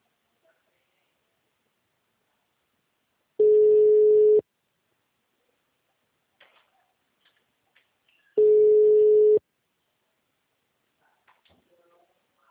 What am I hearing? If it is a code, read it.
Ringback tone (Europe, 425 Hz).
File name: out- -2013- - - .0